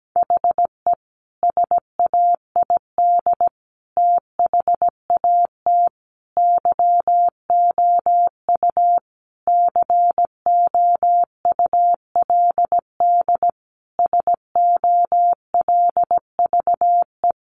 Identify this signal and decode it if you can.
Morse code, heard as a keyed tone, about 17 wpm, 700 Hz. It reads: HE SAID THAT YOU COULD SOLVE
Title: Sentences from Top 1000 Words 17wpm